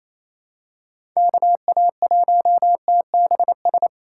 Morse code, heard as a keyed tone, about 28 wpm, 700 Hz. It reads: KA1TBS
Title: USA Callsign DE Callsign 28wpm